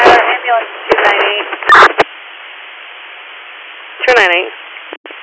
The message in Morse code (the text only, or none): none